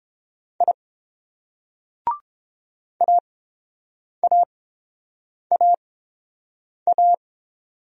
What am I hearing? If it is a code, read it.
Morse code, heard as a keyed tone, about 28 wpm, spaced as for 7 wpm, 700 Hz. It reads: I AAAA